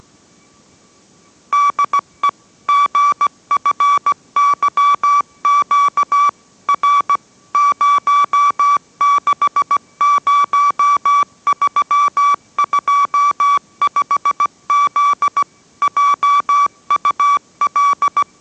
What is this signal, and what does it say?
Morse code, heard as a keyed tone, about 17 wpm, 1150 Hz. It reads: DE GFYQ R 060325Z JUL